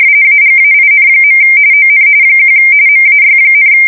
RTTY, about 45.45 baud. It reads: AGAIN. ON 1 MAY,